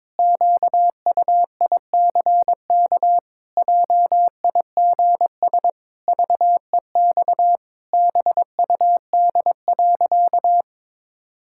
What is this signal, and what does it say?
Morse code, heard as a keyed tone, about 22 wpm, 700 Hz. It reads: QUICK JIGS VEX BUD.
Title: Pangram Challenge v2 - 22wpm